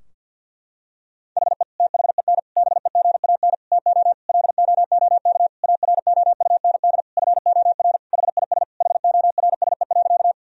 Morse code, heard as a keyed tone, about 50 wpm, 700 Hz. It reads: HE THEN BEGAN TO LOOK AROUND FOR HIS HORSE.
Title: Sentences from Top 1000 Words - Review Part 3 - 50wpm